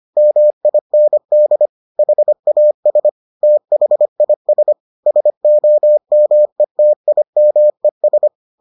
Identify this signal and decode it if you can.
Morse code, heard as a keyed tone, about 25 wpm, 600 Hz. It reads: MIND HAS THIS SOMETIMES